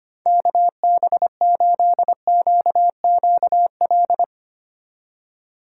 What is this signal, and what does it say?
Morse code, heard as a keyed tone, about 32 wpm, 700 Hz. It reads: KB8QQL